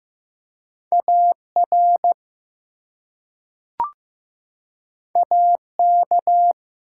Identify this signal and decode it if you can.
Morse code, heard as a keyed tone, about 15 wpm, 700 Hz. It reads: AR AK
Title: US States and Canadian Provinces 15wpm